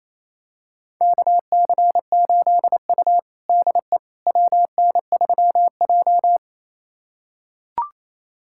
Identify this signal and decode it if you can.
Morse code, heard as a keyed tone, about 28 wpm, 700 Hz. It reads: KC8U DE WN3J
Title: USA Callsign DE Callsign - 4 Characters - 28wpm